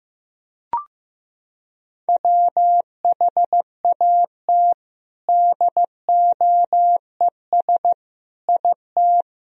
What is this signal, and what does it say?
Morse code, heard as a keyed tone, about 16 wpm, 700 Hz. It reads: WHAT DOES IT
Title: Sentences from Top 1000 Words - Review Part 1 - 15wpm